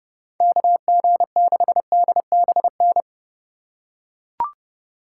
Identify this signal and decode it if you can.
Morse code, heard as a keyed tone, about 31 wpm, 700 Hz. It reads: KG6DBN